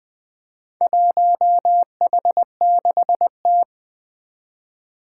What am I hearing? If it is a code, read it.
Morse code, heard as a keyed tone, about 20 wpm, 700 Hz. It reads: 1H6T
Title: Four Challenging Letter-Number Combinations 20wpm